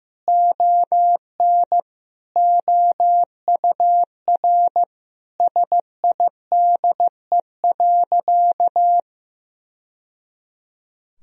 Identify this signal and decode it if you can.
Morse code, heard as a keyed tone, about 16 wpm, 700 Hz. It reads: ON OUR SIDE.
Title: Sentences from Top 1000 Words - Review Part 3 - 15wpm